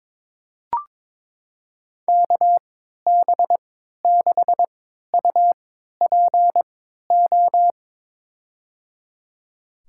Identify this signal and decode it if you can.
Morse code, heard as a keyed tone, about 22 wpm, spaced as for 13 wpm, 700 Hz. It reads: KB6UPO
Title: USA Call Signs - 6 Characters - 3x Character Spacing - 22wpm